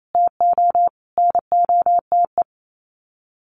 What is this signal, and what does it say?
Morse code, heard as a keyed tone, about 28 wpm, 700 Hz. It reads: TO NOTE
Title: Sets of 5 Words Encore2 28wpm